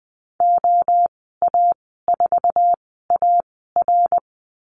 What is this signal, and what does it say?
Morse code, heard as a keyed tone, about 19 wpm, spaced as for 14 wpm, 700 Hz. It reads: OA4AR
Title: 173 Lesson 28g - Number 9 - Call Signs 20wpm